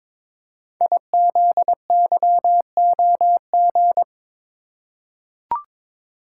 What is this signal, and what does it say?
Morse code, heard as a keyed tone, about 22 wpm, 700 Hz. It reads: IZYOG
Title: Code Groups - 5 Letters - 22wpm